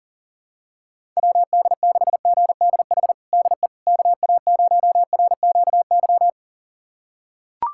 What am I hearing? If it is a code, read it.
Morse code, heard as a keyed tone, about 37 wpm, 700 Hz. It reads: WD6GDH DE KA0RQY
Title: USA Callsign DE Callsign - 6 Characters - 40wpm